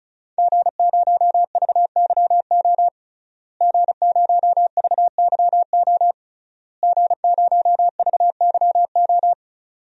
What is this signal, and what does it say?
Morse code, heard as a keyed tone, about 36 wpm, 700 Hz. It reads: G0VYO G0VYO G0VYO
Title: DXCC-100 Call Signs - 5 Characters - Repeated 3x - 35wpm